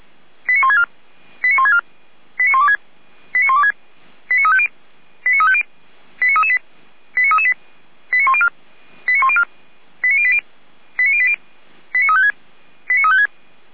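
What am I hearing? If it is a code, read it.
Five-tone selective call (ZVEI), heard as quick runs of five tones, tones of about 70 ms. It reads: 89164 89164 89126 89126 89350 89350 89338 89338 89104 89104 89080 89080 89356 89356
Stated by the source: ZVEI five-tone selective call, code 89701, received via SDR.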